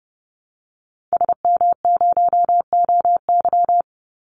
Morse code, heard as a keyed tone, about 30 wpm, 700 Hz. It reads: SM0OY
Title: Call Signs - 30wpm